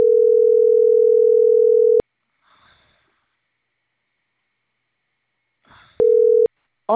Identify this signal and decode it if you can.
Ringback tone (North America), 440 + 480 Hz.